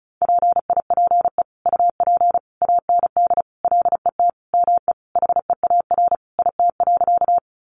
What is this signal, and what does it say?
Morse code, heard as a keyed tone, about 34 wpm, 700 Hz. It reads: PIPE UP AND LET ME HEAR IT.
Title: Sentences from Top 1500 Words 35wpm